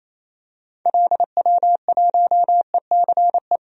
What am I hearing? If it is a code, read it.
Morse code, heard as a keyed tone, about 28 wpm, 700 Hz. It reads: LW1ECE
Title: DXCC-100 Call Signs - 6 Characters - Repeated 3x - 28wpm